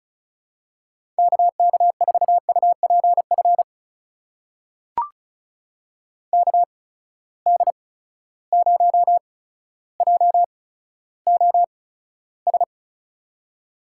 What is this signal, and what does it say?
Morse code, heard as a keyed tone, about 36 wpm, spaced as for 15 wpm, 700 Hz. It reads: KK4UPF KD0JOS
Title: USA Call Signs - 6 Characters - 8x Character Spacing - 35wpm